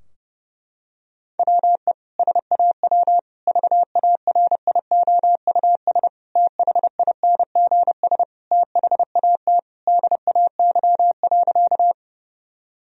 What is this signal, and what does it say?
Morse code, heard as a keyed tone, about 30 wpm, 700 Hz. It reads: WE SAW VARIOUS THINGS THAT DAY.
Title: Sentences from Top 1000 Words - Review Part 4 - 30wpm